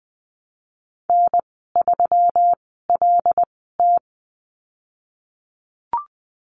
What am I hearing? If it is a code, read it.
Morse code, heard as a keyed tone, about 20 wpm, spaced as for 15 wpm, 700 Hz. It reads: N3LT
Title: 167 Lesson 27g - Number 7 - Call Signs 20wpm